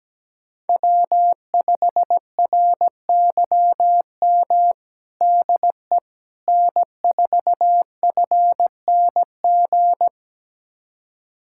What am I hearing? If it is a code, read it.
Morse code, heard as a keyed tone, about 17 wpm, 700 Hz. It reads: W5RYM DE N4FNG
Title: USA Callsign DE Callsign - 5 Characters - 17wpm